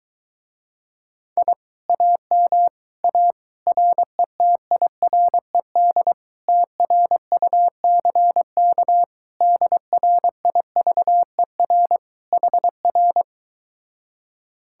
Morse code, heard as a keyed tone, about 22 wpm, 700 Hz. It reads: I AM A RETIRED TRUCK DRIVER HR